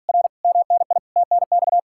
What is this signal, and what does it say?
Morse code, heard as a keyed tone, about 54 wpm, 700 Hz. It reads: R MNI TNX